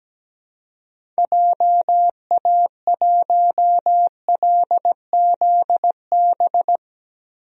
Morse code, heard as a keyed tone, about 17 wpm, 700 Hz. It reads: JA1LZB
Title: DXCC-100 Call Signs - 6 Characters - Repeated 2x - 17wpm